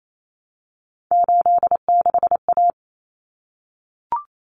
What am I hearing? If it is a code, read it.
Morse code, heard as a keyed tone, about 28 wpm, 700 Hz. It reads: 86A
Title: Common Cut Numbers - 3-Digit 28wpm